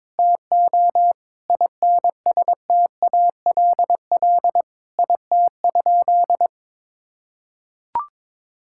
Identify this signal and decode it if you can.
Morse code, heard as a keyed tone, about 21 wpm, 700 Hz. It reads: TO INSTALL IT?
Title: Sentences from Top 1600 Words 22wpm